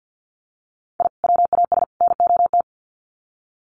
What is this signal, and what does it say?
Morse code, heard as a keyed tone, about 50 wpm, 700 Hz. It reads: I WAS NOT
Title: Sets of 3 Words 50wpm